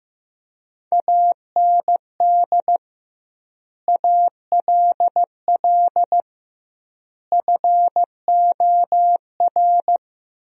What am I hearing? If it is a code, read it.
Morse code, heard as a keyed tone, about 16 wpm, 700 Hz. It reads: AND ALL FOR
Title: 2x Word-Spacing - Sentences from Top 100 Words - 15wpm